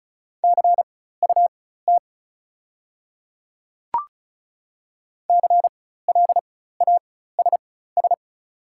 Morse code, heard as a keyed tone, about 34 wpm, spaced as for 16 wpm, 690 Hz. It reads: CUT CLASS
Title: 4x Character Spacing - Top 500 Words - 35wpm